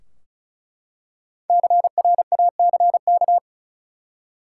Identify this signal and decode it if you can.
Morse code, heard as a keyed tone, about 34 wpm, 700 Hz. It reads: CRACK
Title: Five Letter Words - 35wpm